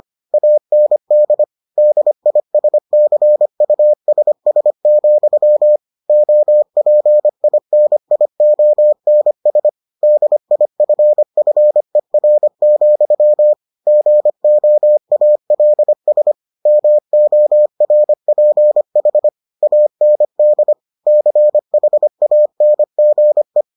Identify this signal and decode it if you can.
Morse code, heard as a keyed tone, about 25 wpm, 600 Hz. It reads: AND DISCUSS, OPINIONS DIFFER, GOALS MORPH AND CHANGE